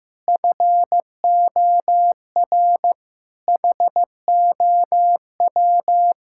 Morse code, heard as a keyed tone, about 15 wpm, 700 Hz. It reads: FOR HOW
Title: Sentences from Top 1000 Words - Review Part 6 - 15wpm